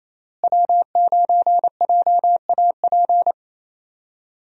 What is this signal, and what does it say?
Morse code, heard as a keyed tone, about 28 wpm, 700 Hz. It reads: W9JAP
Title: USA Call Signs - 5 Characters - Repeated 3x - 28wpm